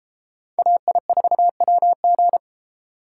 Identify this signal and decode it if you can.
Morse code, heard as a keyed tone, about 32 wpm, 700 Hz. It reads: AI4WG